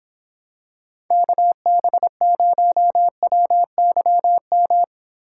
Morse code, heard as a keyed tone, about 26 wpm, 700 Hz. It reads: KB0WYM